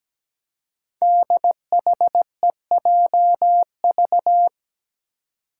Morse code, heard as a keyed tone, about 17 wpm, 700 Hz. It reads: DHEJV
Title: Five Challenging Letter-Number Combinations 17wpm